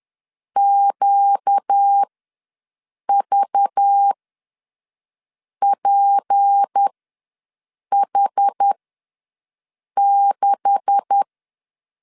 Morse code, heard as a keyed tone, about 10 wpm, spaced as for 5 wpm, 800 Hz. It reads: QVPH6
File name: LEZIONI / 9 286 ws 10wpm 4032